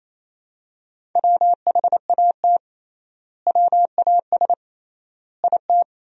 Morse code, heard as a keyed tone, about 28 wpm, 700 Hz. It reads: WHAT WAS IT
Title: 3x Word-Spacing - Sentences from Top 100 Words - 28wpm